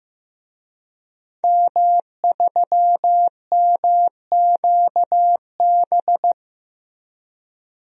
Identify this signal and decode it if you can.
Morse code, heard as a keyed tone, about 15 wpm, 700 Hz. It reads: M3MQB
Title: DXCC-100 Call Signs - 5 Characters - Repeated 3x - 15wpm